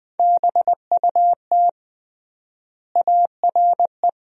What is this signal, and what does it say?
Morse code, heard as a keyed tone, about 21 wpm, 700 Hz. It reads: BUT ARE
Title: 3x Word-Spacing - Sentences from Top 300 Words - 20wpm